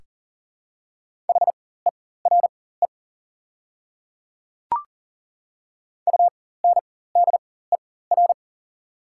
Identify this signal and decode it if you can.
Morse code, heard as a keyed tone, about 40 wpm, spaced as for 19 wpm, 690 Hz. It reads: HERE UNDER
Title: 4x Character Spacing - Top 200 Words - 40wpm